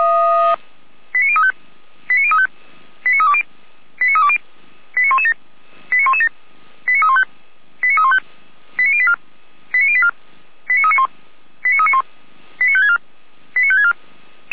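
Five-tone selective call (ZVEI), heard as quick runs of five tones, tones of about 70 ms. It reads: 89035 89035 89320 89320 89117 89117 89315 89315 89084 89084 89391 89391 89564 89564
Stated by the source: Five-tone ZVEI selective call received by SDR.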